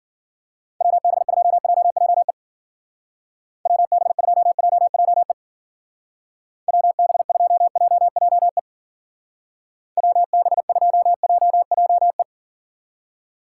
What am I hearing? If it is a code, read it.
Morse code, heard as a keyed tone, about 48 wpm, 700 Hz. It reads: WB2JJE WB2JJE WB2JJE WB2JJE
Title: Speed Racing - USA 2x3 Call Signs 40wpm